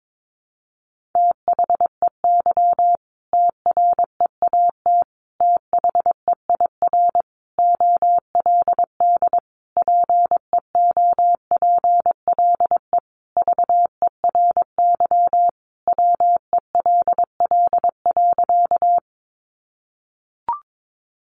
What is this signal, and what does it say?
Morse code, heard as a keyed tone, about 22 wpm, 700 Hz. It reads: THEY TREAT THEIR OLD PEOPLE VERY WELL.
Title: Sentences from Top 1300 Words 22wpm